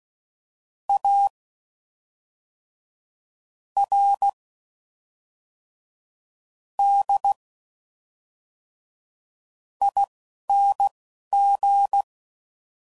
Morse code, heard as a keyed tone, about 16 wpm, spaced as for 5 wpm, 800 Hz. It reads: ARDING